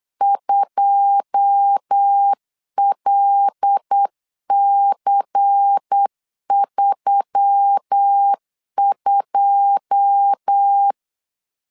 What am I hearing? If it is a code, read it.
Morse code, heard as a keyed tone, about 8 wpm, 800 Hz. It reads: 2LC32
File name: LEZIONI / 9 2 8wpm 3437